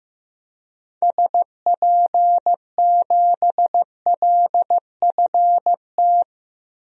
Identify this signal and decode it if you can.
Morse code, heard as a keyed tone, about 15 wpm, 700 Hz. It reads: SP7LFT
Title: DXCC-100 Call Signs - 6 Characters - 15wpm